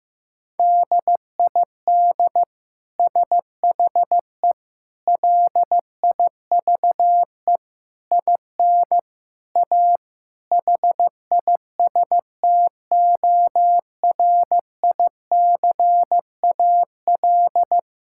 Morse code, heard as a keyed tone, about 15 wpm, 700 Hz. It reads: DID SHE LIVE IN A HISTORICAL
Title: Sentences from Top 1000 Words - Review Part 5 - 15wpm